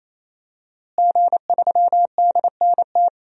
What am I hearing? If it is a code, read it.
Morse code, heard as a keyed tone, about 27 wpm, 700 Hz. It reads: G3DNT
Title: DXCC-100 Call Signs - 5 Characters - Repeated 3x - 28wpm